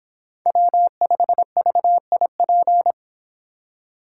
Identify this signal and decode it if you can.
Morse code, heard as a keyed tone, about 30 wpm, 700 Hz. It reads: W5VIP